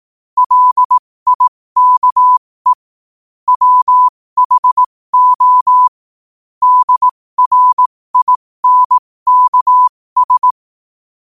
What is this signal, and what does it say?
Morse code, heard as a keyed tone, about 18 wpm, spaced as for 15 wpm, 1000 Hz. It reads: LIKE WHO DRINKS
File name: qotdfiles / 15wpm